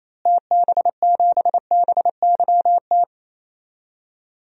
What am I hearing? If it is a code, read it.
Morse code, heard as a keyed tone, about 28 wpm, 700 Hz. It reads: TB7BYT
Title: DXCC-100 Call Signs - 6 Characters - Repeated 3x - 28wpm